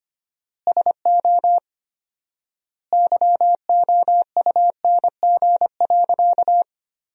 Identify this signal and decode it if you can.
Morse code, heard as a keyed tone, about 25 wpm, 700 Hz. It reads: SO YOUNG.
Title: 4x Word-Spacing - Sentences from Top 400 Words - 25wpm